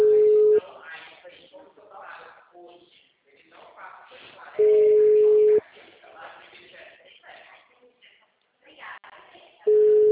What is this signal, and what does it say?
Ringback tone (Europe, 425 Hz), with speech.